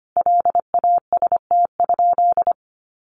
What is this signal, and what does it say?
Morse code, heard as a keyed tone, about 25 wpm, 700 Hz. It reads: LAST?
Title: Sentences from Top 1200 Words 25wpm